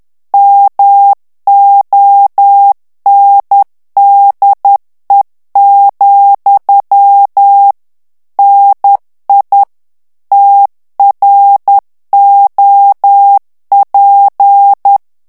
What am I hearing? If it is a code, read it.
Morse code, heard as a keyed tone, about 10 wpm, 800 Hz. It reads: MONDE, NI TROP
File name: morse mp3 / FRANCAIS TEXTE CLAIR 2 VITESSE 10 MOTS MINUTES